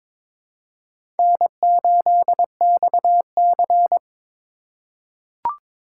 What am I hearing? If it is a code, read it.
Morse code, heard as a keyed tone, about 22 wpm, 700 Hz. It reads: N8XC
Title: USA Call Signs - 4 Characters - 4x Character Spacing - 22wpm